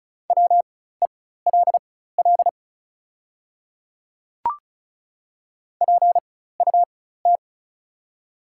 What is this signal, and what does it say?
Morse code, heard as a keyed tone, about 34 wpm, spaced as for 16 wpm, 700 Hz. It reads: WELL PUT
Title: 4x Character Spacing - Top 200 Words - 35wpm